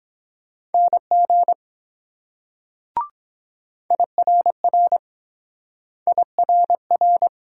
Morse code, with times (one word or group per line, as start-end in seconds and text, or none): 0.7-1.5 NG
3.9-5.0 IRR
6.1-7.3 IRR